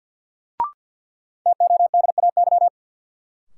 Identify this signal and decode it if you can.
Morse code, heard as a keyed tone, about 53 wpm, 700 Hz. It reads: TODAY
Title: Sets of 5 Words Encore - New Word Drill - 50wpm